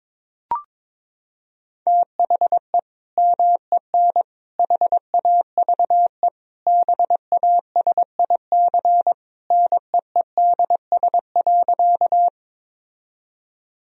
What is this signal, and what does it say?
Morse code, heard as a keyed tone, about 22 wpm, 700 Hz. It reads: THE MEN HAVE BASIC NEEDS.
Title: Sentences from Top 1000 Words - Review Part 3 - 22wpm